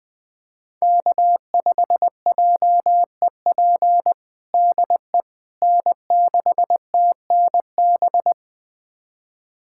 Morse code, heard as a keyed tone, about 20 wpm, 700 Hz. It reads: K5JEP DE N6TNB
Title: USA Callsign DE Callsign - 5 Characters - 20wpm